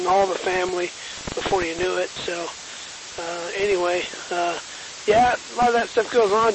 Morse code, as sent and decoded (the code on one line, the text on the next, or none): none